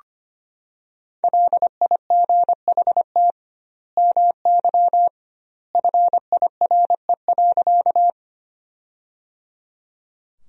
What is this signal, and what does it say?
Morse code, heard as a keyed tone, about 25 wpm, 700 Hz. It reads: LIGHT MY FIRE.
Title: 2x Word-Spacing - Sentences from Top 400 Words - 25wpm